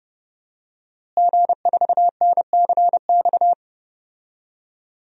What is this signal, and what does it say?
Morse code, heard as a keyed tone, about 30 wpm, 700 Hz. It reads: G4NCX
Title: DXCC-100 Call Signs - 5 Characters - Repeated 2x - 30wpm